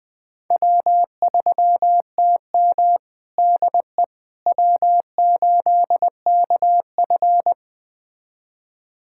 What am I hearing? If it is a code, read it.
Morse code, heard as a keyed tone, about 20 wpm, 700 Hz. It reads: W3TM DE W8KF
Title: USA Callsign DE Callsign - 4 Characters - 20wpm